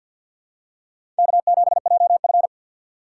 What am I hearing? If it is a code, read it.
Morse code, heard as a keyed tone, about 50 wpm, 700 Hz. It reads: K7JF
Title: USA Call Signs - 4 Characters - 2x Character Spacing - 50wpm